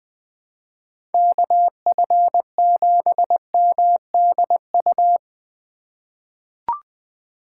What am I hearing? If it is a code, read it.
Morse code, heard as a keyed tone, about 20 wpm, 700 Hz. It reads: KF7MDU
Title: USA Call Signs - 6 Characters - 2x Character Spacing - 20wpm